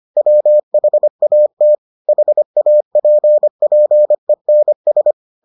Morse code, heard as a keyed tone, about 25 wpm, 600 Hz. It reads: WHAT HAPPENS